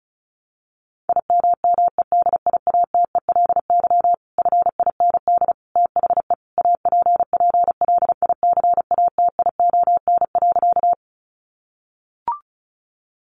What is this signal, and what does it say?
Morse code, heard as a keyed tone, about 34 wpm, 700 Hz. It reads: IMMEDIATELY FIND THE APPLICATION.
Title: Sentences from Top 1200 Words 35wpm